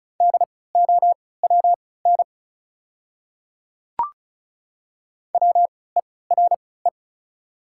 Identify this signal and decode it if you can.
Morse code, heard as a keyed tone, about 36 wpm, spaced as for 20 wpm, 700 Hz. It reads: DOWN WERE
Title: 3x Character Spacing - Top 100 Words - 35wpm